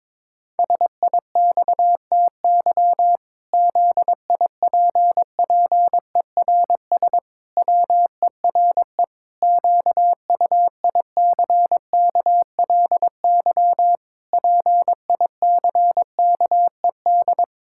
Morse code, heard as a keyed tone, about 22 wpm, 700 Hz. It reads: SIXTY ZIPPERS WERE QUICKLY PICKED
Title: Pangram Challenge v2 - 22wpm